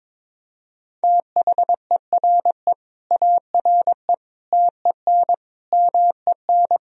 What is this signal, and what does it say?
Morse code, heard as a keyed tone, about 22 wpm, 700 Hz. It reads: THERE ARE TEN MEN
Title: Sentences from Top 1000 Words - Review Part 2 - 22wpm